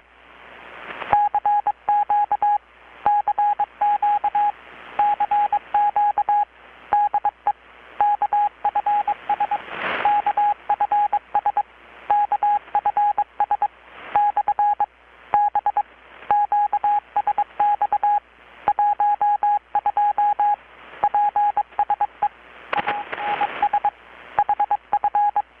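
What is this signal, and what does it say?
Morse code, heard as a keyed tone, about 22 wpm, 830 Hz, with radio noise. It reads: CQ CQ CQ DE KFS KFS KFS / B QSX 12 PSE ANS HF
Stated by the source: Morse from KPH's Night of Nights XX maritime radiotelegraphy event, received in Chiapas, Mexico about 3,800 km away on a Kenwood R-600 with a grounded 10 m long-wire antenna.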